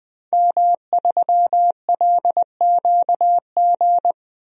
Morse code, heard as a keyed tone, about 20 wpm, 700 Hz. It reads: M3LQG
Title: DXCC-100 Call Signs - 5 Characters - Repeated 2x - 20wpm